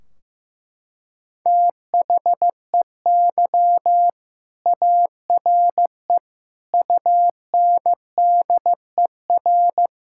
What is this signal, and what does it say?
Morse code, heard as a keyed tone, about 16 wpm, 700 Hz. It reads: THEY ARE UNDER
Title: Sentences from Top 1600 Words 15wpm